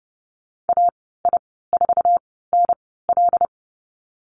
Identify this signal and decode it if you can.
Morse code, heard as a keyed tone, about 30 wpm, spaced as for 17 wpm, 700 Hz. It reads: AI4NL
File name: Lesson 14g - Number 2 - Call Signs 30wpm